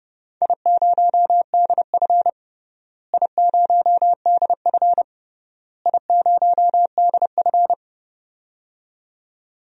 Morse code, heard as a keyed tone, about 30 wpm, 700 Hz. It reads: I0DF I0DF I0DF
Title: DXCC-100 Call Signs - 4 Characters - Repeated 3x - 30wpm